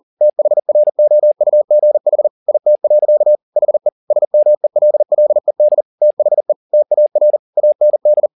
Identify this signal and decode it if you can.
Morse code, heard as a keyed tone, about 40 wpm, 600 Hz. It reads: THROUGH IT. HE SMELLED THE TAR AND